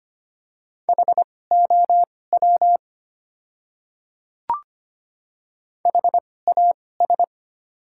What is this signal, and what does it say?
Morse code, heard as a keyed tone, about 25 wpm, spaced as for 18 wpm, 690 Hz. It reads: HOW HAS
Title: 2x Character Spacing - Top 100 Words - 25wpm